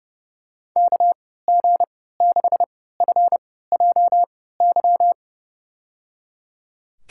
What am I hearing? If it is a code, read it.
Morse code, heard as a keyed tone, about 30 wpm, spaced as for 17 wpm, 700 Hz. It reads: KG6FJY
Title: USA Call Signs - 6 Characters - 3x Character Spacing - 30wpm